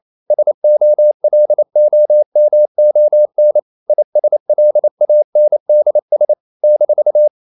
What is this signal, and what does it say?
Morse code, heard as a keyed tone, about 28 wpm, 600 Hz. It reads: SOLOMON ISLANDS -
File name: CW / Practice / NationalCapitals28